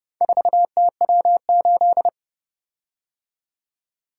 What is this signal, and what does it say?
Morse code, heard as a keyed tone, about 29 wpm, 700 Hz. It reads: 4TW8